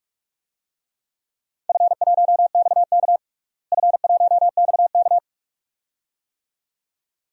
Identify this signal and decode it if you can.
Morse code, heard as a keyed tone, about 46 wpm, 700 Hz. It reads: F1XK F1XK
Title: DXCC-100 Call Signs - 4 Characters - Repeated 2x - 45wpm